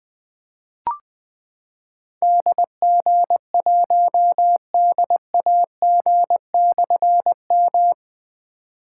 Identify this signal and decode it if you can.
Morse code, heard as a keyed tone, about 20 wpm, 700 Hz. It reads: DG1DAG/M
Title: Difficult Call Signs 20wpm